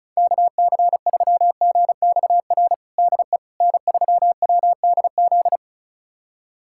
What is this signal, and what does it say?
Morse code, heard as a keyed tone, about 36 wpm, 700 Hz. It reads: KC3GXR DE N3WDZ
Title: USA Callsign DE Callsign 35wpm